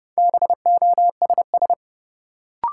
Morse code, heard as a keyed tone, about 32 wpm, 700 Hz. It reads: BOSS
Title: Sentences from Top 700 Words - New Word Drill - 30wpm